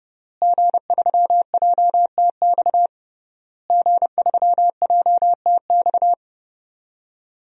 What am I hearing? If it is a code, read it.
Morse code, heard as a keyed tone, about 32 wpm, 700 Hz. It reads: G3JTX G3JTX
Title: DXCC-100 Call Signs - 5 Characters - Repeated 3x - 30wpm